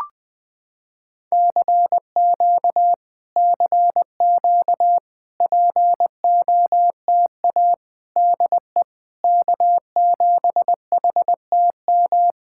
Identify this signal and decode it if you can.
Morse code, heard as a keyed tone, about 20 wpm, 700 Hz. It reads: CQ CQ POTA DE K7HTM
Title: POTA QSO Practice - 20wpm